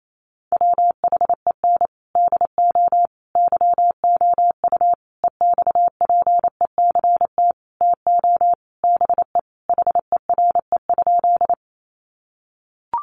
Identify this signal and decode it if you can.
Morse code, heard as a keyed tone, about 28 wpm, 700 Hz. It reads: WHEN DO YOU EXPECT TO BE HERE?
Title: Sentences from Top 1100 Words 28wpm